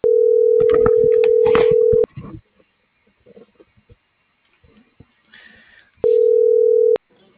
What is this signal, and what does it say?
Ringback tone (North America), 440 + 480 Hz.